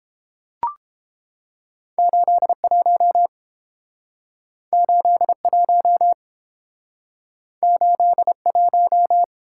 Morse code, heard as a keyed tone, about 32 wpm, 700 Hz. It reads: 81 81 81